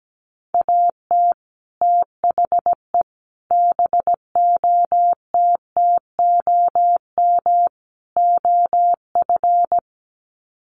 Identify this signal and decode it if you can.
Morse code, heard as a keyed tone, about 17 wpm, 700 Hz. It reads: AT THE BOTTOM OF
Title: Sets of 4 Words 17wpm